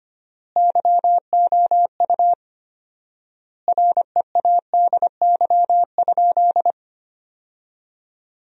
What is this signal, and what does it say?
Morse code, heard as a keyed tone, about 25 wpm, 700 Hz. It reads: YOU READY?
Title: 4x Word-Spacing - Sentences from Top 400 Words - 25wpm